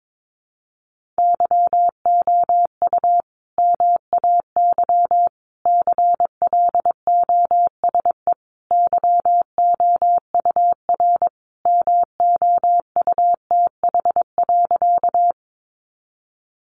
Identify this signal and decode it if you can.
Morse code, heard as a keyed tone, about 22 wpm, 700 Hz. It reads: YOU MAY CLOSE YOUR MOUTH.
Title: Sentences from Top 1500 Words 22wpm